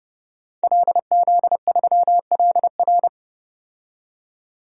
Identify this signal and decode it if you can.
Morse code, heard as a keyed tone, about 31 wpm, 700 Hz. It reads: LZ3LR
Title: DXCC-100 Call Signs - 5 Characters - 30wpm